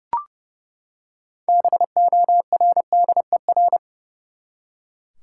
Morse code, heard as a keyed tone, about 31 wpm, 700 Hz. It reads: BORDER
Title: Six Letter Words - 30wpm